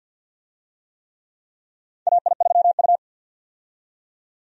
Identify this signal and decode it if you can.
Morse code, heard as a keyed tone, about 52 wpm, 700 Hz. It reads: AI3U
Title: USA Call Signs - 4 Characters - 50wpm